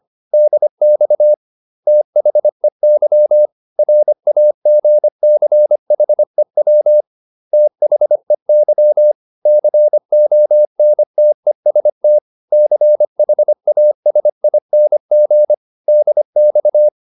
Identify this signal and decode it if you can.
Morse code, heard as a keyed tone, about 25 wpm, 600 Hz. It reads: DX THEY RAGCHEW THEY CONTEST CHASING DX